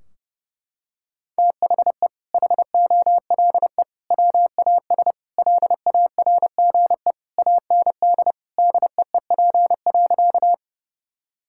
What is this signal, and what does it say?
Morse code, heard as a keyed tone, about 30 wpm, 700 Hz. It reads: THE HOLE WAS LARGE AND DEEP.
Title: Sentences from Top 1600 Words 30wpm